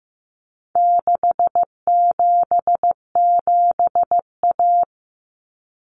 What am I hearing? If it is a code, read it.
Morse code, heard as a keyed tone, about 15 wpm, 700 Hz. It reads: 677A
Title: Common Cut Numbers - 4-Digit 15wpm